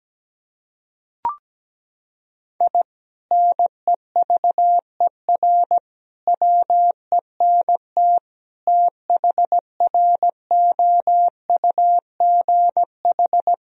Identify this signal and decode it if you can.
Morse code, heard as a keyed tone, about 17 wpm, 700 Hz. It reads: I NEVER WENT THROUGH
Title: Sentences from Top 1000 Words - Review Part 3 - 17wpm